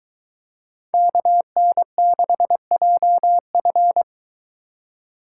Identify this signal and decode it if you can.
Morse code, heard as a keyed tone, about 23 wpm, 700 Hz. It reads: KN6JF